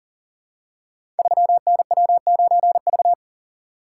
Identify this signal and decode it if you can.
Morse code, heard as a keyed tone, about 38 wpm, 700 Hz. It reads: 3NW9V
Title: Code Groups - 5 Letter:Numbers - 40wpm